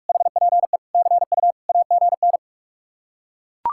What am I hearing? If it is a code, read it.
Morse code, heard as a keyed tone, about 49 wpm, 700 Hz. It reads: HPE CU AGN